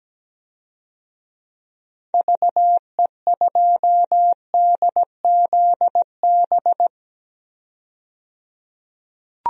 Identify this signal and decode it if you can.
Morse code, heard as a keyed tone, about 17 wpm, 700 Hz. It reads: VE2DZB
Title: DXCC-100 Call Signs - 6 Characters - 17wpm